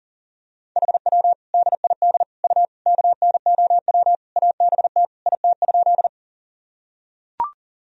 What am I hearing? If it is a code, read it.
Morse code, heard as a keyed tone, about 40 wpm, 700 Hz. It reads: HW DID U KNOW ABT IT?
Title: Sentences from Top 100 Words and Abbreviations 40wpm